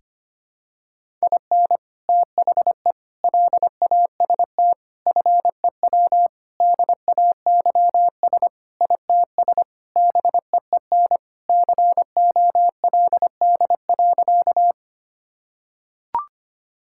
Morse code, heard as a keyed tone, about 25 wpm, 700 Hz. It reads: IN THE LAST FEW DAYS ITS BEEN COLD.
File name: practice / 25wpm / Sentences with 3-Word Phrases - 25wpm